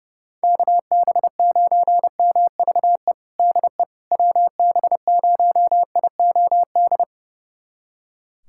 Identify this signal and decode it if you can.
Morse code, heard as a keyed tone, about 30 wpm, 700 Hz. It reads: KB9MVE DE WB0IOD
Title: USA Callsign DE Callsign 30wpm